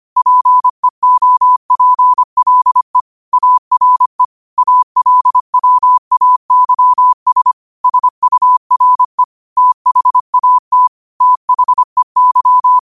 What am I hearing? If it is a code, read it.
Morse code, heard as a keyed tone, about 25 wpm, 1000 Hz. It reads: PEOPLE ARE ALWAYS SURE THAT THEY